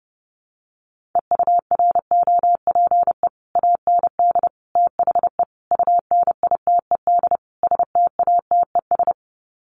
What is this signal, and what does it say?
Morse code, heard as a keyed tone, about 30 wpm, 700 Hz. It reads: EUROPE AND THE UNITED STATES
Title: Sets of 5 Words Encore2 30wpm